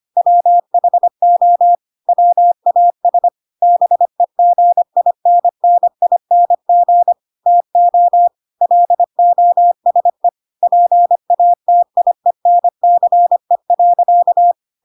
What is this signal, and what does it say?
Morse code, heard as a keyed tone, about 25 wpm, 700 Hz. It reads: WHO WAS BEGINNING TO LOSE PATIENCE.
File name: Pinocchio / Pinocchio ch4 25 25 ik1hge